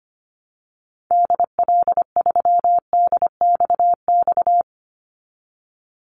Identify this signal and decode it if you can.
Morse code, heard as a keyed tone, about 25 wpm, 700 Hz. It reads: DL3DXX
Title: Call Signs Repeated 3x - 25wpm